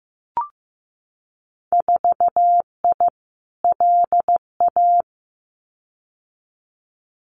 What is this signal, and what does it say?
Morse code, heard as a keyed tone, about 15 wpm, 700 Hz. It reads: 4I LA
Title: Winter Field Day Exchanges Encore 15wpm